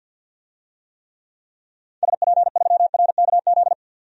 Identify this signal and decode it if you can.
Morse code, heard as a keyed tone, about 51 wpm, 700 Hz. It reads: SP3RKZ
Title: DXCC-100 Call Signs - 6 Characters - 50wpm